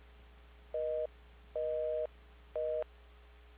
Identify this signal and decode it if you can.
Busy tone (North America), 480 + 620 Hz.
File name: recordings / PortlandUnion Millennium